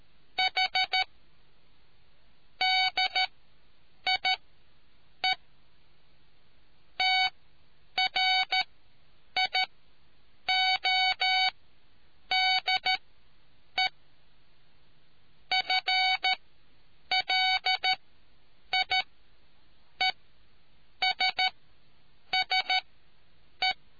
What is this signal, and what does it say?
Morse code, heard as a keyed tone, about 13 wpm, spaced as for 7 wpm, 750 Hz. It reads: H DIE TRIODE FLIESSE